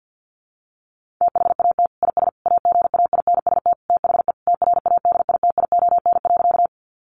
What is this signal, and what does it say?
Morse code, heard as a keyed tone, about 50 wpm, 700 Hz. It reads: THAT IS AGAINST THE TRADITION.
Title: Sentences from Top 1500 Words 50wpm